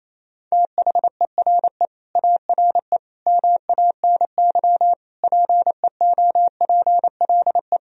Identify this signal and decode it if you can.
Morse code, heard as a keyed tone, about 28 wpm, 700 Hz. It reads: THERE ARE MANY PEOPLE
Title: Sentences from Top 1000 Words - Review Part 1 - 28wpm